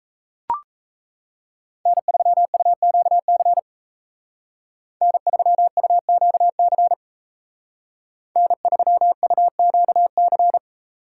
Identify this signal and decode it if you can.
Morse code, heard as a keyed tone, about 36 wpm, 700 Hz. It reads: N3UQC N3UQC N3UQC